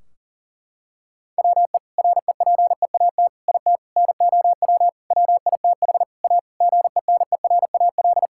Morse code, heard as a keyed tone, about 40 wpm, 700 Hz. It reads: WE REPEAT IT NOW WITH A GENERAL
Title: Sentences from Top 1600 Words 40wpm